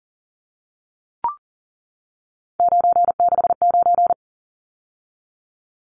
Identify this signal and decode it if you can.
Morse code, heard as a keyed tone, about 40 wpm, 700 Hz. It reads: 969
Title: Numbers - 3-Digit 40wpm